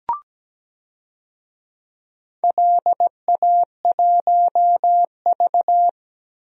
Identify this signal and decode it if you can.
Morse code, heard as a keyed tone, about 17 wpm, 700 Hz. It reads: LA1V
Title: DXCC-100 Call Signs - 4 Characters - Repeated 2x - 17wpm